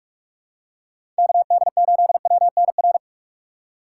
Morse code, heard as a keyed tone, about 48 wpm, 700 Hz. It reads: KD8WNR
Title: USA Call Signs - 6 Characters - 4x Character Spacing - 45wpm